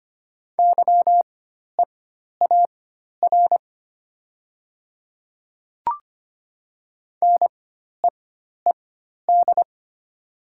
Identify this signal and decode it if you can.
Morse code, heard as a keyed tone, about 25 wpm, spaced as for 12 wpm, 700 Hz. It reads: YEAR NEED
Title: 4x Character Spacing - Top 300 Words - 25wpm